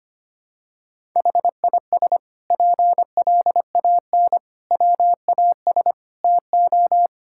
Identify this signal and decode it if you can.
Morse code, heard as a keyed tone, about 25 wpm, 700 Hz. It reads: HIS PLAN WAS TO
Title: Sentences from Top 1000 Words - Review Part 6 - 25wpm